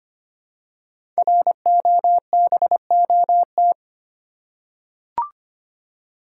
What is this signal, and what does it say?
Morse code, heard as a keyed tone, about 25 wpm, 700 Hz. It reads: ROBOT
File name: practice / 25wpm / Easy Words with Hints - 25wpm